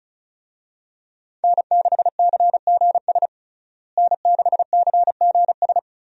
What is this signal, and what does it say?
Morse code, heard as a keyed tone, about 36 wpm, 700 Hz. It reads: N6CGS N6CGS
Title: USA Call Signs - 5 Characters - Repeated 3x - 35wpm